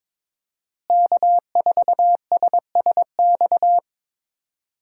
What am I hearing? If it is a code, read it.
Morse code, heard as a keyed tone, about 22 wpm, 700 Hz. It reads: K4SSX
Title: USA Call Signs - 5 Characters - Repeated 3x - 22wpm